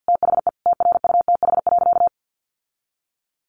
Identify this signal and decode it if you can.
Morse code, heard as a keyed tone, about 50 wpm, 700 Hz. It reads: THE TRUTH.
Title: Sentences from Top 1400 Words 50wpm